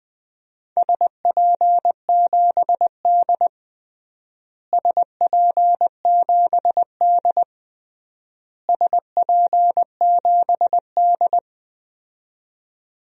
Morse code, heard as a keyed tone, about 20 wpm, 700 Hz. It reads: SP7D SP7D SP7D
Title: DXCC-100 Call Signs - 4 Characters - Repeated 3x - 20wpm